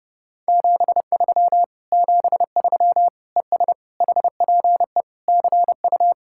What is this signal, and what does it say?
Morse code, heard as a keyed tone, about 40 wpm, 700 Hz. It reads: 73 73 ES HPE CU